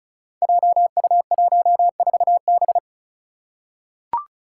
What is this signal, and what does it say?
Morse code, heard as a keyed tone, about 36 wpm, 700 Hz. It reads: JU14B
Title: Five Challenging Letter-Number Combinations 35wpm